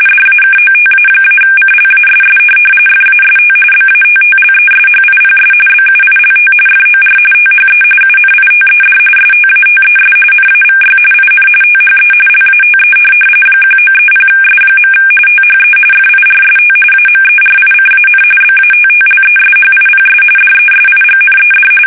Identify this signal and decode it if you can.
RTTY, about 45.45 baud. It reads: DAKOTA-CLASS FAST BATTLESHIPS BUILT FOR THE UNITED STATES NAVY IN THE LATE 1930S. THE FIRST AMERICAN BATTLESHIPS DESIGNED
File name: fsk850 / USS-Massachusetts